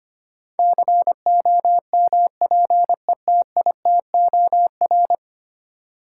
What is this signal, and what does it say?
Morse code, heard as a keyed tone, about 25 wpm, 700 Hz. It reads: COMPETITOR
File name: 10 Letter Words - First 3 Letters Given - 25wpm